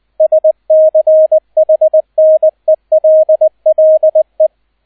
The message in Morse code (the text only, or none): SCHNELLE